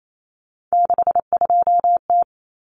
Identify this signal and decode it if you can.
Morse code, heard as a keyed tone, about 28 wpm, 700 Hz. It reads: 62T
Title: Common Cut Numbers - 3-Digit 28wpm